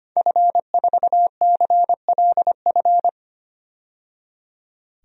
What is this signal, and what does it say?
Morse code, heard as a keyed tone, about 25 wpm, 700 Hz. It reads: F4CLF
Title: DXCC-100 Call Signs - 5 Characters - 25wpm